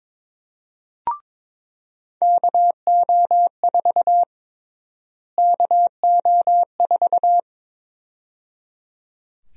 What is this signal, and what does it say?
Morse code, heard as a keyed tone, about 21 wpm, 700 Hz. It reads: KO4 KO4
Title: USA Call Sign Prefixes - Repeated 2x - 22wpm